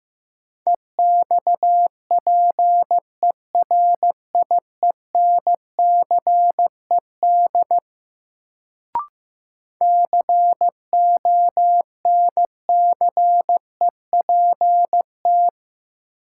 Morse code, heard as a keyed tone, about 16 wpm, 700 Hz. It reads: EXPERIENCED CONCEPT
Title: Sentences from Top 1100 Words - New Word Drill - 15wpm